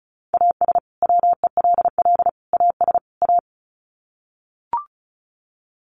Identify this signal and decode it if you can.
Morse code, heard as a keyed tone, about 34 wpm, 700 Hz. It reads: AS WELL AS A
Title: Sets of 4 Words 35wpm